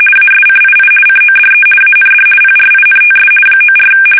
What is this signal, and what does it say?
RTTY, about 50 baud. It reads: ASCII CODE AND PRINTS 94
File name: fsk850-50B / teletype-corp